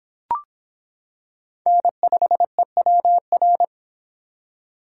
Morse code, heard as a keyed tone, about 25 wpm, 700 Hz. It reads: N5EWR